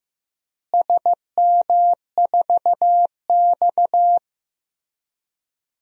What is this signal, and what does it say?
Morse code, heard as a keyed tone, about 15 wpm, 700 Hz. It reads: SM4X
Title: DXCC-100 Call Signs - 4 Characters - Repeated 3x - 15wpm